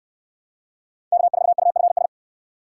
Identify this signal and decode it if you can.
Morse code, heard as a keyed tone, about 63 wpm, 700 Hz. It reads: K4UFS